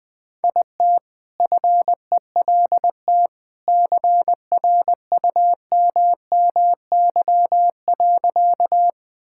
Morse code, heard as a keyed tone, about 20 wpm, 700 Hz. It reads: IT FELT CRUMMY.
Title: All-Age Appropriate Jokes - 20wpm